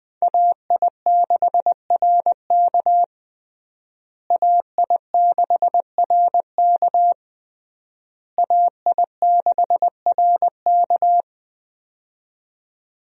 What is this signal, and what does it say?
Morse code, heard as a keyed tone, about 21 wpm, 700 Hz. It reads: AI6RK AI6RK AI6RK
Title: USA Call Signs - 5 Characters - Repeated 3x - 20wpm